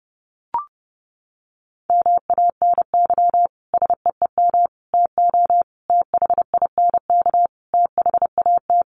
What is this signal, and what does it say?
Morse code, heard as a keyed tone, about 30 wpm, 700 Hz. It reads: MANY SEEM TO THINK THAT
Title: Sentences from Top 1400 Words 30wpm